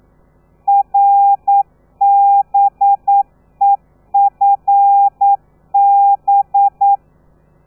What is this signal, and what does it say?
Morse code, heard as a keyed tone, about 9 wpm, 800 Hz. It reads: RBEFB
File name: telegrafia / i2tfj / Lez13 40